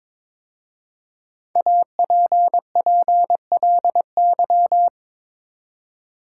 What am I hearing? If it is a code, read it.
Morse code, heard as a keyed tone, about 23 wpm, 700 Hz. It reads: APPLY
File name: 5 Letter Words - First 3 Letters Given - 22wpm